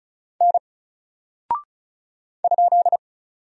Morse code, heard as a keyed tone, about 34 wpm, 700 Hz. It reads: N ?
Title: Single Character (Letter:Number:Punctuation:Prosign) 35wpm